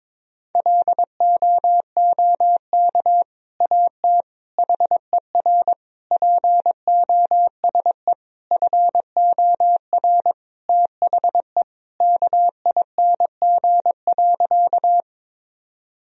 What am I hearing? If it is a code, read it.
Morse code, heard as a keyed tone, about 22 wpm, 700 Hz. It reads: LOOK AT HER POSE FOR THE KING.
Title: Sentences from Top 1000 Words - Review Part 2 - 22wpm